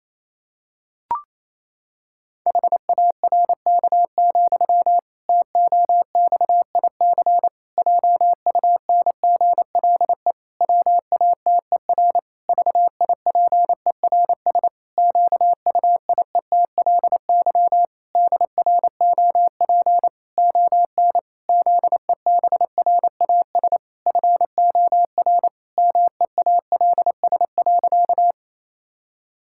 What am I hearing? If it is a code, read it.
Morse code, heard as a keyed tone, about 27 wpm, 700 Hz. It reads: HARK, TOXIC JUNGLE WATER VIPERS QUIETLY DROP ON ZEBRAS FOR MEALS.
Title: Pangram Challenge v2 - 28wpm